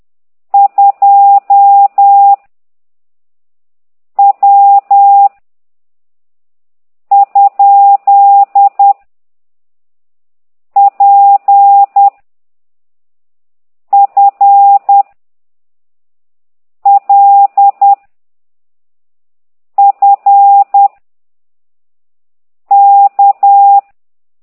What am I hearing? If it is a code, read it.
Morse code, heard as a keyed tone, about 10 wpm, 800 Hz. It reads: 2 W ? P F L F K